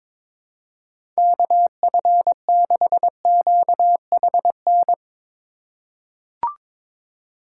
Morse code, heard as a keyed tone, about 22 wpm, 700 Hz. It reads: KF6QHN
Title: USA Call Signs - 6 Characters - 3x Character Spacing - 22wpm